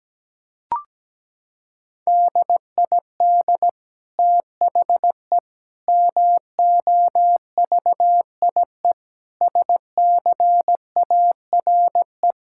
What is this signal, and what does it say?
Morse code, heard as a keyed tone, about 17 wpm, 700 Hz. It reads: DID THE MOVIE SCARE
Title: Sentences from Top 1000 Words - Review Part 5 - 17wpm